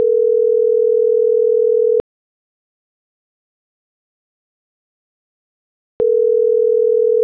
Ringback tone (North America), 440 + 480 Hz.